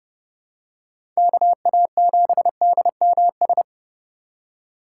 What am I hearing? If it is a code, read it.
Morse code, heard as a keyed tone, about 30 wpm, 700 Hz. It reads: KA7DMS
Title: USA Call Signs - 6 Characters - 4x Character Spacing - 30wpm